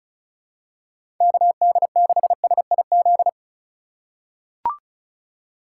Morse code, heard as a keyed tone, about 36 wpm, 700 Hz. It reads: KD6SIZ
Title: USA Call Signs - 6 Characters - 2x Character Spacing - 35wpm